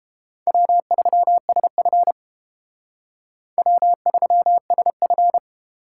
Morse code, heard as a keyed tone, about 30 wpm, 700 Hz. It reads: W3SF W3SF